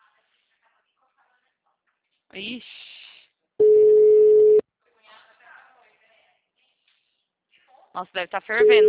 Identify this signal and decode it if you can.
Ringback tone (Europe, 425 Hz), with speech.